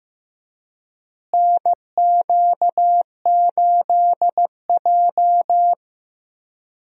Morse code, heard as a keyed tone, about 15 wpm, 700 Hz. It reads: NQ8J
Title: USA Call Signs - 4 Characters - Repeated 2x - 15wpm